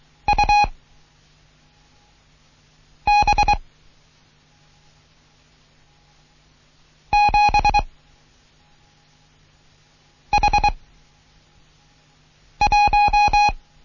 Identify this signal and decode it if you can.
Morse code, heard as a keyed tone, about 23 wpm, 830 Hz. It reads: U B 7 H 1
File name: Morsen / Kurs A 16